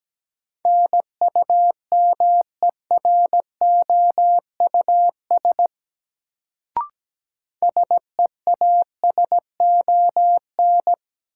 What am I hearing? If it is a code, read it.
Morse code, heard as a keyed tone, about 17 wpm, 700 Hz. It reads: NUMEROUS SEASON
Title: Sentences from Top 1400 Words - New Word Drill - 17wpm